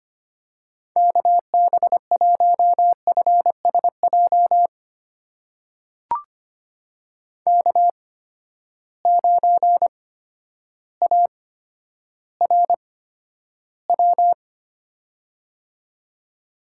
Morse code, heard as a keyed tone, about 25 wpm, spaced as for 11 wpm, 700 Hz. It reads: KB1FSJ K9ARW
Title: USA Call Signs - 8x Character Spacing - 25wpm